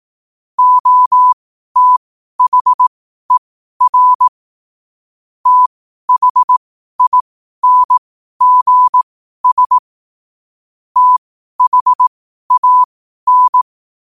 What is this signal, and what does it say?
Morse code, heard as a keyed tone, about 18 wpm, spaced as for 12 wpm, 1000 Hz. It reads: OTHER THINGS THAN
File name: qotdfiles / 12wpm